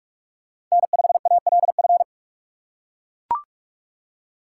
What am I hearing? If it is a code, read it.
Morse code, heard as a keyed tone, about 44 wpm, 700 Hz. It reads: N5ALF